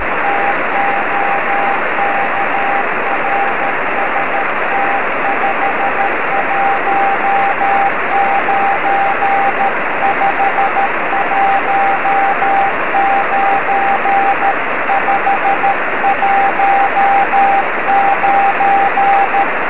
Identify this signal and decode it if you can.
Morse code, heard as a keyed tone, about 13 wpm, 790 Hz, with radio noise. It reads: TOK RST 519 519 519
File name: 0452z dk3mn 13cm